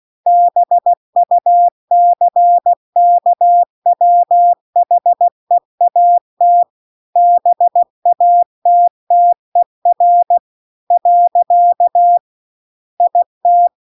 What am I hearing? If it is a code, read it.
Morse code, heard as a keyed tone, about 16 wpm, 700 Hz. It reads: BUCKWHEAT BATTER . IT
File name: CW-The-Big-Twoheaded-River-2-16WPM